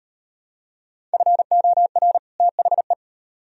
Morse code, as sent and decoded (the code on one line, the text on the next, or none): ..-. --- .-. / - .... .
FOR THE